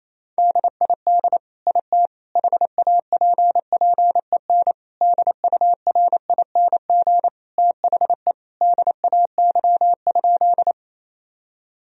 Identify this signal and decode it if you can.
Morse code, heard as a keyed tone, about 28 wpm, 700 Hz. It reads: DID IT HAPPEN DURING THE DAY?
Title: Sentences from Top 1000 Words - Review Part 2 - 28wpm